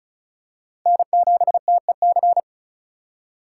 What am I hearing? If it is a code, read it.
Morse code, heard as a keyed tone, about 36 wpm, 700 Hz. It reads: N7TEC
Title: USA Call Signs - 5 Characters - 4x Character Spacing - 35wpm